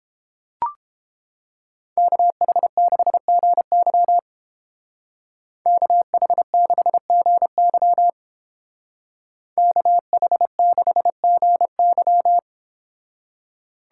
Morse code, heard as a keyed tone, about 29 wpm, 700 Hz. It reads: KH6GY KH6GY KH6GY